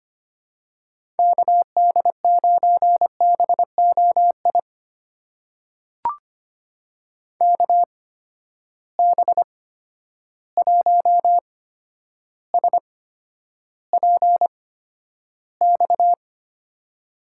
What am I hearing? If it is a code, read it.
Morse code, heard as a keyed tone, about 25 wpm, spaced as for 11 wpm, 700 Hz. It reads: KD9BOI KB1SPX
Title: USA Call Signs - 6 Characters - 8x Character Spacing - 25wpm